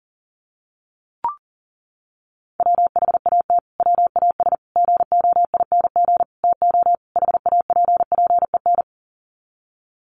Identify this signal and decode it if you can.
Morse code, heard as a keyed tone, about 40 wpm, 700 Hz. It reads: WHAT WAS GOING TO HAPPEN
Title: Sets of 5 Words Encore 40wpm